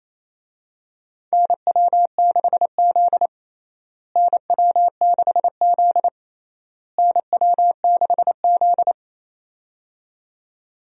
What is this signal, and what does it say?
Morse code, heard as a keyed tone, about 27 wpm, 700 Hz. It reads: NW6Z NW6Z NW6Z
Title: USA Call Signs - 4 Characters - Repeated 3x - 28wpm